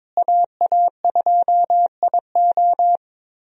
Morse code, heard as a keyed tone, about 22 wpm, 700 Hz. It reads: AA2IO
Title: USA Call Signs - 5 Characters - Repeated 3x - 22wpm